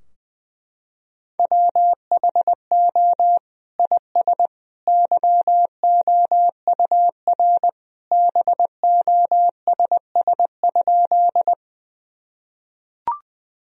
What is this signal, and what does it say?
Morse code, heard as a keyed tone, about 20 wpm, 700 Hz. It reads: WHO IS YOUR BOSS?
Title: Sentences from Top 1000 Words - Review Part 4 - 20wpm